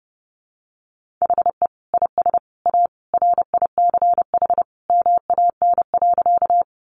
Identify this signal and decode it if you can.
Morse code, heard as a keyed tone, about 30 wpm, 700 Hz. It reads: HE IS A RICH MAN.
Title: Sentences from Top 1100 Words 30wpm